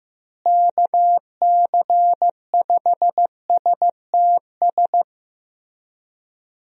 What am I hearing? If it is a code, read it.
Morse code, heard as a keyed tone, about 15 wpm, 700 Hz. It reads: KC5STS
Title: USA Call Signs - 6 Characters - Repeated 3x - 15wpm